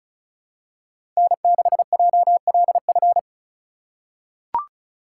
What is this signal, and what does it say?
Morse code, heard as a keyed tone, about 34 wpm, 700 Hz. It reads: N6JLF
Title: USA Call Signs - 5 Characters - 2x Character Spacing - 35wpm